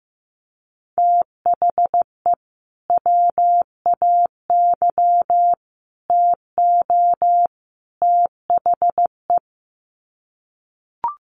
Morse code, heard as a keyed tone, about 15 wpm, 700 Hz. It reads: THE WAY TO THE
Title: Sets of 4 Words 15wpm